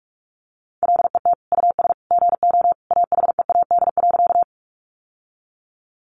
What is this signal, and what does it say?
Morse code, heard as a keyed tone, about 44 wpm, 700 Hz. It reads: LET US GO AHEAD.
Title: Sentences from Top 1100 Words 45wpm